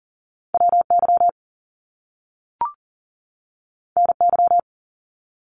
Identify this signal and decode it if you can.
Morse code, heard as a keyed tone, about 38 wpm, 700 Hz. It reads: WY NY
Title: US States 40wpm